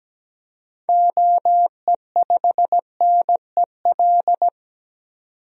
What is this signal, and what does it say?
Morse code, heard as a keyed tone, about 17 wpm, 700 Hz. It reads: OE5NEL
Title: DXCC-100 Call Signs - 6 Characters - 17wpm